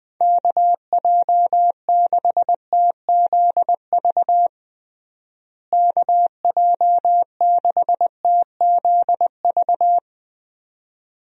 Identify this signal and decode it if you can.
Morse code, heard as a keyed tone, about 20 wpm, 700 Hz. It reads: KJ6TZV KJ6TZV
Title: USA Call Signs - 6 Characters - Repeated 2x - 20wpm